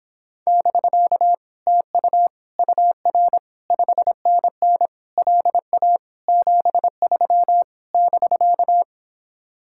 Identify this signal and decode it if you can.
Morse code, heard as a keyed tone, about 26 wpm, 700 Hz. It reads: <BK> TU UR 5NN LA 73 <BK>